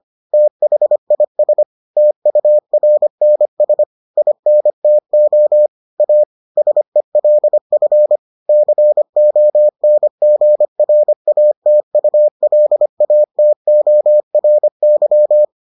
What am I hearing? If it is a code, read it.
Morse code, heard as a keyed tone, about 25 wpm, 600 Hz. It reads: THIS TURNS INTO A SELF CONGRATULATORY